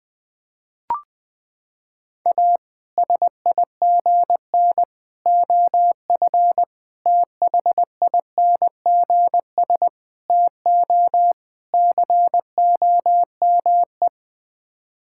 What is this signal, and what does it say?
Morse code, heard as a keyed tone, about 20 wpm, 700 Hz. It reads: A SIGN OF THINGS TO COME
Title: Sets of 6 Words - 20wpm